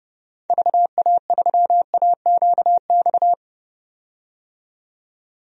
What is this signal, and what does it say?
Morse code, heard as a keyed tone, about 32 wpm, 700 Hz. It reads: VA3AQX
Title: DXCC-100 Call Signs - 6 Characters - 30wpm